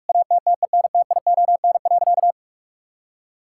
Morse code, heard as a keyed tone, about 41 wpm, 700 Hz. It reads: ATTENTION.
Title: Sentences from Top 1000 Words - Review Part 6 - 45wpm